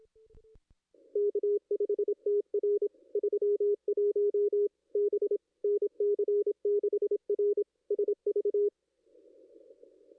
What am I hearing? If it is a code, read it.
Morse code, heard as a keyed tone, about 27 wpm, 410 Hz. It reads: K5TR 31 B NC6R SV
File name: contests / ss-cw-k5tr-op-n5rz -2200